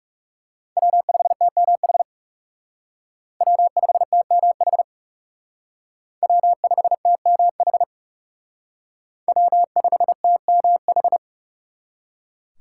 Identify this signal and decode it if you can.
Morse code, heard as a keyed tone, about 36 wpm, 700 Hz. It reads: W5TMH W5TMH W5TMH W5TMH
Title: Speed Racing - USA Call Signs 30wpm